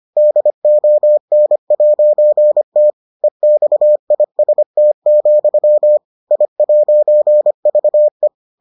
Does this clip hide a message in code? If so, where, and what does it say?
Yes: Morse code from 0.2 to 8.3 s, heard as a keyed tone, about 25 wpm, 600 Hz. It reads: DON'T EXIST, I'VE